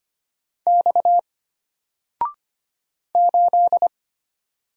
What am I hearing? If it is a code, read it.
Morse code, heard as a keyed tone, about 25 wpm, 700 Hz. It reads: X 8
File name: Single-Letter-Number-25wpm